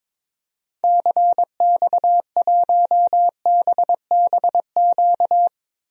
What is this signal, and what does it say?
Morse code, heard as a keyed tone, about 22 wpm, 700 Hz. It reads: CX1BBQ
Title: DXCC-100 Call Signs - 6 Characters - 22wpm